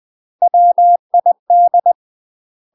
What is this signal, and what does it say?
Morse code, heard as a keyed tone, about 20 wpm, 700 Hz. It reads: WID
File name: UK MASTER QSOs TRAINING / 20WPM / UK MASTER QSOs 0035